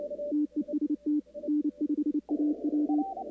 Morse code, heard as a keyed tone, about 26 wpm, 280 Hz. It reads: TEST N5AW
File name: contests / 2019-cw-wpx-kt5j-k5pi- -2330 0-45